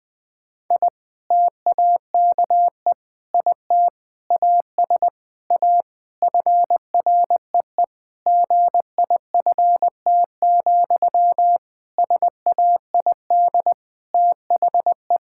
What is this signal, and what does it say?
Morse code, heard as a keyed tone, about 19 wpm, 700 Hz. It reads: I TAKE IT AS A FREE GIFT, SAID THE
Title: Sentences from Top 1000 Words - Review Part 6 - 20wpm